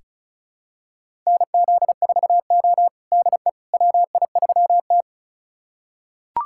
Morse code, heard as a keyed tone, about 34 wpm, 700 Hz. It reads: NZ4O DE WI3T
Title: USA Callsign DE Callsign - 4 Characters - 35wpm